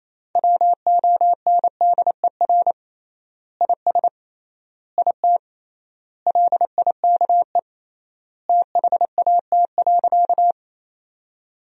Morse code, heard as a keyed tone, about 28 wpm, 700 Hz. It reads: WONDER IS IT LIKE THAT.
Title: 3x Word-Spacing - Sentences from Top 500 Words - 28wpm